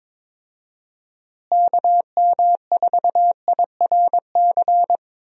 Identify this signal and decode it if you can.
Morse code, heard as a keyed tone, about 22 wpm, 700 Hz. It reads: KM4IRC
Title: USA Call Signs - 6 Characters - Repeated 3x - 22wpm